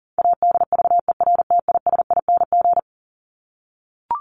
Morse code, heard as a keyed tone, about 40 wpm, 700 Hz. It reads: ADVERTISING
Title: Top 751-1000 Words 40wpm